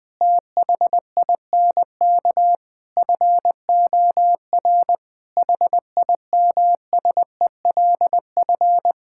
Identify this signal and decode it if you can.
Morse code, heard as a keyed tone, about 20 wpm, 700 Hz. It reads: THINK FOR HIMSELF
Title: Sentences from Top 1600 Words 20wpm